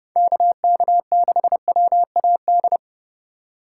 Morse code, heard as a keyed tone, about 32 wpm, 700 Hz. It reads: KK6WAD
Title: USA Call Signs - 6 Characters - 4x Character Spacing - 30wpm